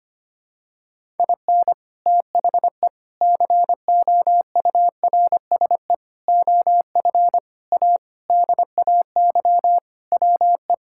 Morse code, heard as a keyed tone, about 25 wpm, 700 Hz. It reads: IN THE COURSE OF A DAY WE
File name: practice / 25wpm / Sentences with 3-Word Phrases - 25wpm